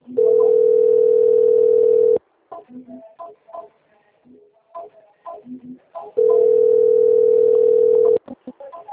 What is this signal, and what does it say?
Ringback tone (North America), 440 + 480 Hz.